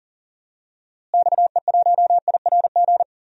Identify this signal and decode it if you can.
Morse code, heard as a keyed tone, about 37 wpm, 700 Hz. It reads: XE1IRG
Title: DXCC-100 Call Signs - 6 Characters - Repeated 2x - 40wpm